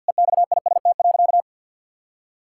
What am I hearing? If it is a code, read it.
Morse code, heard as a keyed tone, about 49 wpm, 700 Hz. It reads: EXIST.
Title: Sentences from Top 1000 Words - Review Part 5 - 50wpm